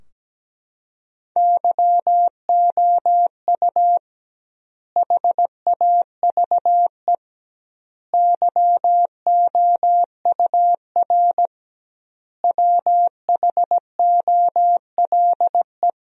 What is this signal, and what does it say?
Morse code, heard as a keyed tone, about 17 wpm, 700 Hz. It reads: YOU HAVE YOUR WHOLE
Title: 2x Word-Spacing - Sentences from Top 400 Words - 17wpm